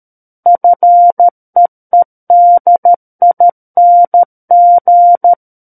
Morse code, heard as a keyed tone, about 13 wpm, 700 Hz. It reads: FEEDING
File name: Book One / 04-Chapter 4 - 13 WPM